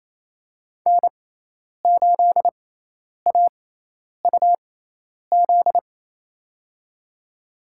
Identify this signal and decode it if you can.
Morse code, heard as a keyed tone, about 28 wpm, spaced as for 10 wpm, 700 Hz. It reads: N8AUZ
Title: USA Call Signs - 6x Character Spacing - 28wpm